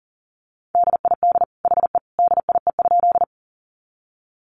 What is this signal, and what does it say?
Morse code, heard as a keyed tone, about 40 wpm, 700 Hz. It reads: DID HE DIE?
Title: Sentences from Top 1300 Words 40wpm